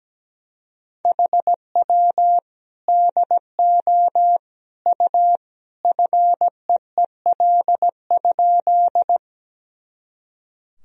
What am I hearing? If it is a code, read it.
Morse code, heard as a keyed tone, about 17 wpm, 700 Hz. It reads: HW DO U FEEL?
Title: Sentences from Top 400 Words and Abbreviations 17wpm